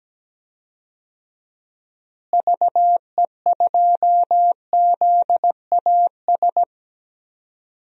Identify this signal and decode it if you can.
Morse code, heard as a keyed tone, about 17 wpm, 700 Hz. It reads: VE2ZAS
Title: DXCC-100 Call Signs - 6 Characters - Repeated 2x - 17wpm